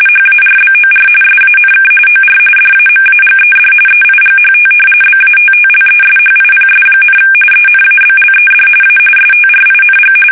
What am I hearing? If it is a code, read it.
RTTY, about 45.45 baud. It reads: HARBOR SANK OR DAMAGED EIGHT BATTLESHIPS, ENABLED A STRING
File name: fsk850 / battleshipnj